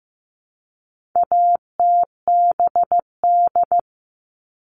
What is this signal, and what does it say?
Morse code, heard as a keyed tone, about 15 wpm, 700 Hz. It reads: ATBD
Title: All Cut Numbers - 4-Digit 15wpm